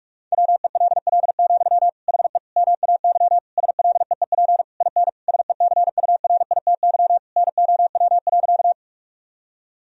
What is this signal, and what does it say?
Morse code, heard as a keyed tone, about 46 wpm, 700 Hz. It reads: WELL, HE MAY SLEEP IN SECURITY NOW.
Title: Sentences from Top 1000 Words - Review Part 4 - 45wpm